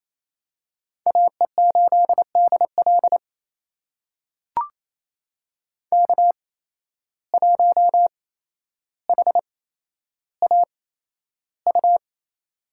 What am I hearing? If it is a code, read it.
Morse code, heard as a keyed tone, about 28 wpm, spaced as for 12 wpm, 700 Hz. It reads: AE8DL K1HAU
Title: USA Call Signs - 8x Character Spacing - 28wpm